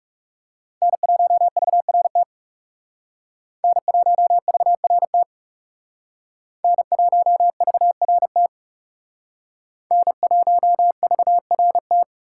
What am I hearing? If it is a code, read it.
Morse code, heard as a keyed tone, about 38 wpm, 700 Hz. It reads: N1VRT N1VRT N1VRT N1VRT